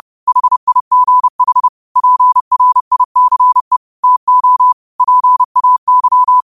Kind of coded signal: Morse code